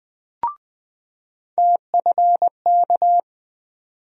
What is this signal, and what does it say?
Morse code, heard as a keyed tone, about 20 wpm, 700 Hz. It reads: TFK
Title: USA Call Sign Postfixes - 3 Letters 20wpm